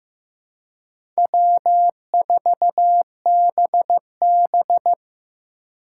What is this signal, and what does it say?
Morse code, heard as a keyed tone, about 15 wpm, 700 Hz. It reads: W4BB
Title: USA Call Signs - 4 Characters - 3x Character Spacing - 15wpm